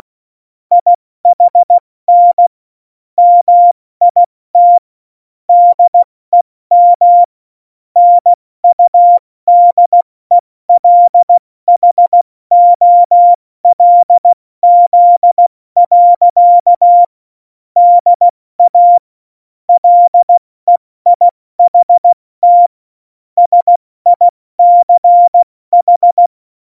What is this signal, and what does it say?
Morse code, heard as a keyed tone, about 16 wpm, 700 Hz. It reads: IHN MIT DEM NUDELHOLZ. DA LEIHT SICH
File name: Morsekurs / Witze / Witz 21 Juli 2020, 70 F80